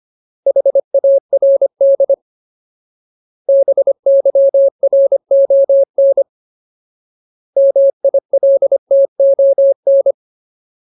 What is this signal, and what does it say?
Morse code, heard as a keyed tone, about 25 wpm, 550 Hz. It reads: HARD BYRON MILTON
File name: Ultimate Word List / 25WPM Ultimate